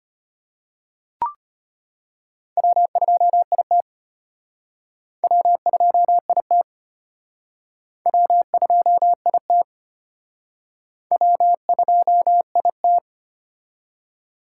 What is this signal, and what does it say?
Morse code, heard as a keyed tone, about 31 wpm, 700 Hz. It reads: W2IT W2IT W2IT W2IT